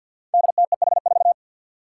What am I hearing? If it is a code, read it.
Morse code, heard as a keyed tone, about 48 wpm, 700 Hz. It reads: DTEH4
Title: Five Challenging Letter-Number Combinations 50wpm